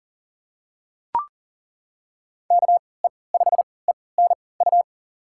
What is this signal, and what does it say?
Morse code, heard as a keyed tone, about 40 wpm, spaced as for 23 wpm, 700 Hz. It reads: KE5ENU
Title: USA Call Signs - 6 Characters - 3x Character Spacing - 40wpm